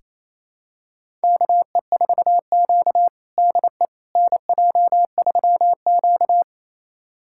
Morse code, heard as a keyed tone, about 28 wpm, 700 Hz. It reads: KE4Q DE NJ3Q
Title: USA Callsign DE Callsign - 4 Characters - 28wpm